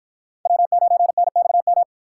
Morse code, heard as a keyed tone, about 53 wpm, 700 Hz. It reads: W9NXM